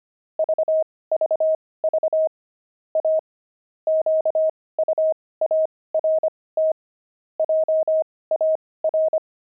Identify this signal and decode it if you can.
Morse code, heard as a keyed tone, about 25 wpm, spaced as for 18 wpm, 630 Hz. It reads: VVV A QUART JAR
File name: pangram-fs-18-cs-25-oxide